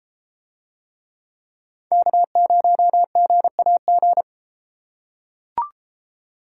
Morse code, heard as a keyed tone, about 31 wpm, 700 Hz. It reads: K0GAG